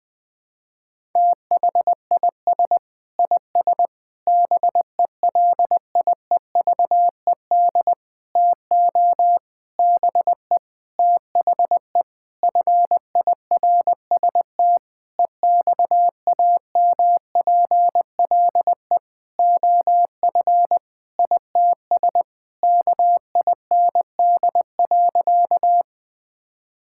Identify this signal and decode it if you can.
Morse code, heard as a keyed tone, about 20 wpm, 700 Hz. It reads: THIS IS BELIEVED TO BE THE FIRST EXAMPLE OF ITS KIND.
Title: Sentences with 6-Word Phrases - 20wpm